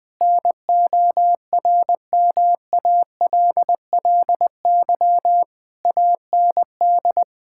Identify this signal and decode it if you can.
Morse code, heard as a keyed tone, about 20 wpm, 700 Hz. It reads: NORMALLY AND
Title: Sentences from Top 1000 Words - Review Part 6 - 20wpm